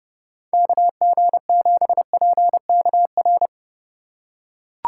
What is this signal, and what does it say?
Morse code, heard as a keyed tone, about 30 wpm, 700 Hz. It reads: KG7PKR
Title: USA Call Signs - 3x Character Spacing - 30wpm